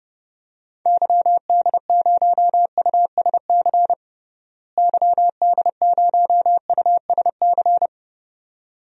Morse code, heard as a keyed tone, about 30 wpm, 700 Hz. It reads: YD0USC YD0USC
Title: DXCC-100 Call Signs - 6 Characters - Repeated 2x - 30wpm